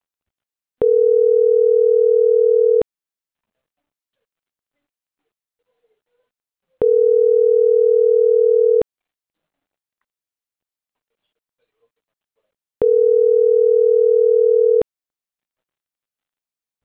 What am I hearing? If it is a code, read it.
Ringback tone (North America), 440 + 480 Hz.